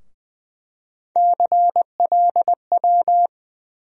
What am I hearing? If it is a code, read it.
Morse code, heard as a keyed tone, about 20 wpm, 700 Hz. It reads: CLW
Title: USA Call Sign Postfixes - 3 Letters 20wpm